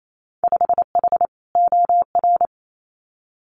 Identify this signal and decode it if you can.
Morse code, heard as a keyed tone, about 28 wpm, 700 Hz. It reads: 5H OR
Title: Winter Field Day Exchanges Encore 28wpm